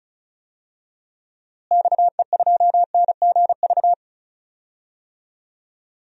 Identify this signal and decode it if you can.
Morse code, heard as a keyed tone, about 36 wpm, 700 Hz. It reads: XE2NGV
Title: DXCC-100 Call Signs - 6 Characters - Repeated 3x - 35wpm